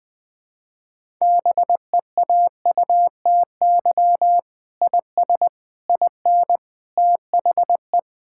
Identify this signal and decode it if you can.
Morse code, heard as a keyed tone, about 21 wpm, 700 Hz. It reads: BEAUTY IS IN THE
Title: Sentences with 8-Word Phrases - 20wpm